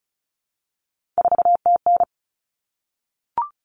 Morse code, heard as a keyed tone, about 34 wpm, 700 Hz. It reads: 4TN
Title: All Cut Numbers - 3-Digit 35wpm